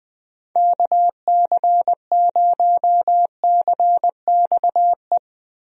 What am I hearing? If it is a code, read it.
Morse code, heard as a keyed tone, about 20 wpm, 700 Hz. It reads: KC0CXE
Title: USA Call Signs - 6 Characters - Repeated 2x - 20wpm